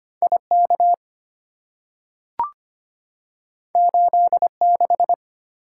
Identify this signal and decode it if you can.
Morse code, heard as a keyed tone, about 25 wpm, 700 Hz. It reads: IK 86
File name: Two Letter-Number Combinations - 25wpm